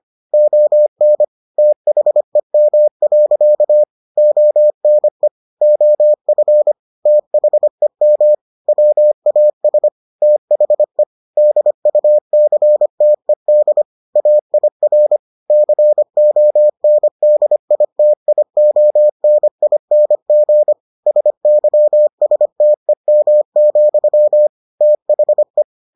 Morse code, heard as a keyed tone, about 25 wpm, 600 Hz. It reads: ON THEM. ONE OF THEM WAS THE DUCTED AIR CONDITIONING SYSTEM, THE